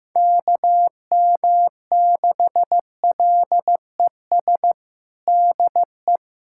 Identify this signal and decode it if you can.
Morse code, heard as a keyed tone, about 15 wpm, 700 Hz. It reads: KM6LES DE
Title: USA Callsign DE Callsign - 6 Characters - 15wpm